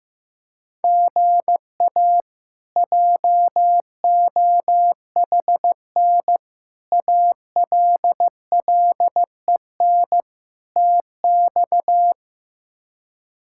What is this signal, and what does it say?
Morse code, heard as a keyed tone, about 16 wpm, 700 Hz. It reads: GA JOHN ALLEN TX
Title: K1USN SST QSO Practice - 15wpm